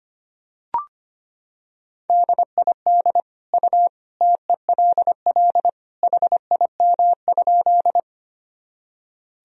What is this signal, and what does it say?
Morse code, heard as a keyed tone, about 25 wpm, 700 Hz. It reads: DID U TELL HIM?
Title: Sentences from Top 200 Words and Abbreviations 25wpm